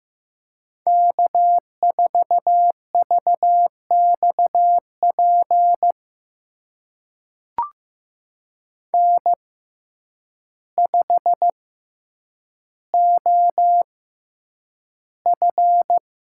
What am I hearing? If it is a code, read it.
Morse code, heard as a keyed tone, about 15 wpm, spaced as for 8 wpm, 700 Hz. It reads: K4VXP N5OF
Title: USA Call Signs - 5 Characters - 6x Character Spacing - 15wpm